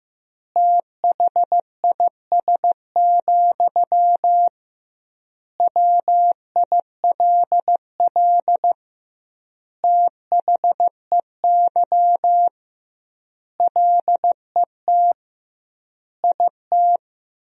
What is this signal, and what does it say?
Morse code, heard as a keyed tone, about 15 wpm, 700 Hz. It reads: THIS, WILL THEY LET IT
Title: 2x Word-Spacing - Sentences from Top 400 Words - 15wpm